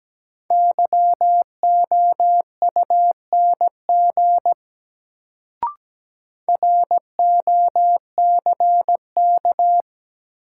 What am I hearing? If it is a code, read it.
Morse code, heard as a keyed tone, about 17 wpm, 700 Hz. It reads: YOUNG ROCK
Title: Sentences from Top 400 Words - New Word Drill - 17wpm